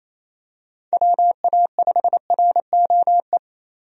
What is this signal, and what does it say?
Morse code, heard as a keyed tone, about 27 wpm, 700 Hz. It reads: WA5ROE
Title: USA Call Signs - 6 Characters - Repeated 3x - 28wpm